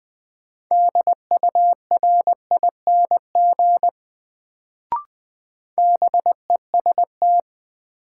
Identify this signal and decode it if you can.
Morse code, heard as a keyed tone, about 20 wpm, 700 Hz. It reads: DURING BEST
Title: Sets of 2 Words - New Word Drill - 20wpm